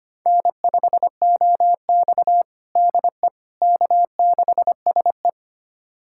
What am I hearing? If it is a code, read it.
Morse code, heard as a keyed tone, about 25 wpm, 700 Hz. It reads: N5OX DE K6SE
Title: USA Callsign DE Callsign - 4 Characters - 25wpm